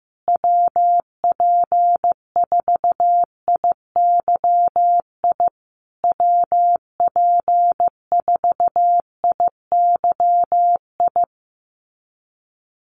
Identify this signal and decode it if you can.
Morse code, heard as a keyed tone, about 15 wpm, 700 Hz. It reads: WP4IYI WP4IYI
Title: USA Call Signs Repeated 3x 15wpm